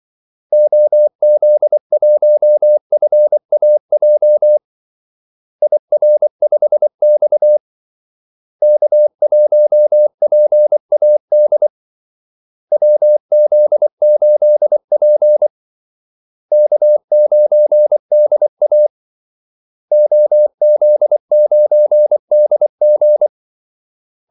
Morse code, heard as a keyed tone, about 24 wpm, 600 Hz. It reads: OZ1FAJ IR5X K1PAD WZ8P K9DA OZ9DG
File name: Callsigns-24wpm